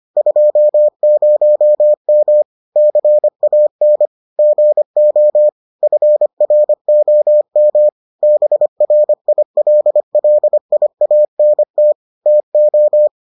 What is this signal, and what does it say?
Morse code, heard as a keyed tone, about 25 wpm, 600 Hz. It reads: 20M CAN GO FROM BRILLIANT TO